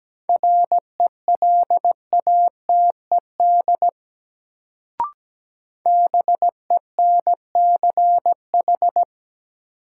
Morse code, heard as a keyed tone, about 17 wpm, 700 Hz. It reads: RELATED BENCH
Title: Sentences from Top 1400 Words - New Word Drill Alt - 17wpm